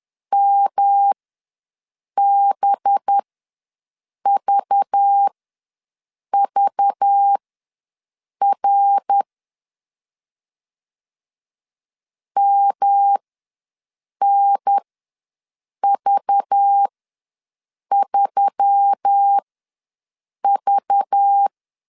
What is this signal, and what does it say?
Morse code, heard as a keyed tone, about 10 wpm, spaced as for 6 wpm, 800 Hz. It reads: MBVVR MNV3V
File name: LEZIONI / 5 LBVC / 5 LBV ws 10wpm 3645